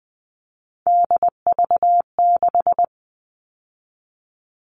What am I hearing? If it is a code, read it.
Morse code, heard as a keyed tone, about 20 wpm, 700 Hz. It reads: DV6
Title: All Cut Numbers - 3-Digit 20wpm